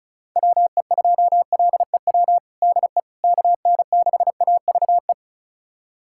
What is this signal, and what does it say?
Morse code, heard as a keyed tone, about 34 wpm, 690 Hz. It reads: WE2LEW DE KN6AVE
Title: USA Callsign DE Callsign - 6 Characters - 35wpm